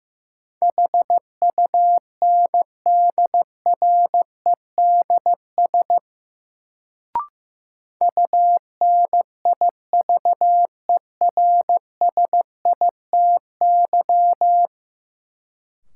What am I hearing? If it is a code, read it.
Morse code, heard as a keyed tone, about 15 wpm, 700 Hz. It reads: HUNDREDS UNIVERSITY
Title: Sets of 4 Words - New Word Drill - 15wpm